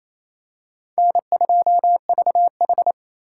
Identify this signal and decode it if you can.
Morse code, heard as a keyed tone, about 29 wpm, 700 Hz. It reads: N2VH
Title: USA Call Signs - 4 Characters - 2x Character Spacing - 28wpm